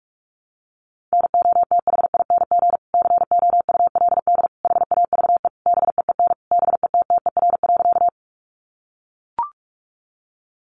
Morse code, heard as a keyed tone, about 44 wpm, 700 Hz. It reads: NOTHING COULD HAVE BEEN BETTER.
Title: Sentences from Top 500 Words 45wpm